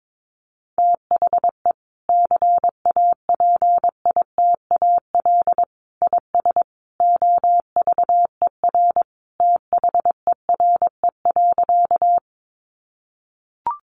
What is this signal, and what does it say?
Morse code, heard as a keyed tone, about 22 wpm, 700 Hz. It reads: THE CAPITAL IS OVER THERE.
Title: Sentences from Top 1000 Words 22wpm